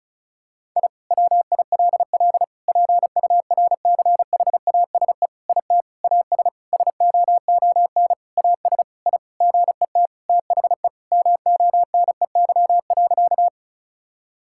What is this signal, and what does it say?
Morse code, heard as a keyed tone, about 34 wpm, 700 Hz. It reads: I WILL PURCHASE IT AS SOON AS I GET THE MONEY.
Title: Sentences from Top 1000 Words - Review Part 5 - 35wpm